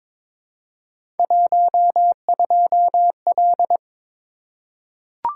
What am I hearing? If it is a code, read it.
Morse code, heard as a keyed tone, about 22 wpm, 700 Hz. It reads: 12L
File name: practice / 22wpm / Three Letter-Number Combinations - 22wpm